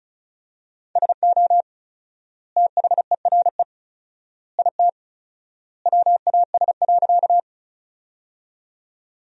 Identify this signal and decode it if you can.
Morse code, heard as a keyed tone, about 36 wpm, 700 Hz. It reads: SO THERE IT WAS.
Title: 4x Word-Spacing - Sentences from Top 100 Words - 35wpm